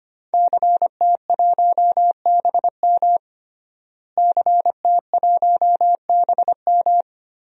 Morse code, heard as a keyed tone, about 25 wpm, 700 Hz. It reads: CT1BM CT1BM
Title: DXCC-100 Call Signs - 5 Characters - Repeated 2x - 25wpm